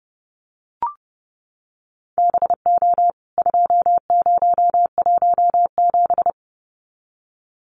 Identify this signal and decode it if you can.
Morse code, heard as a keyed tone, about 30 wpm, 700 Hz. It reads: BO 2017
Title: CWT Exchange 30wpm